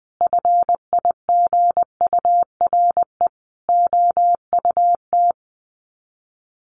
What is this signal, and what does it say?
Morse code, heard as a keyed tone, about 20 wpm, 700 Hz. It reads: FIGURE OUT
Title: Sets of 4 Words 20wpm